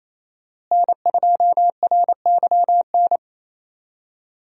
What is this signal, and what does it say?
Morse code, heard as a keyed tone, about 28 wpm, 700 Hz. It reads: N2RYN
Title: USA Call Signs - 5 Characters - Repeated 3x - 28wpm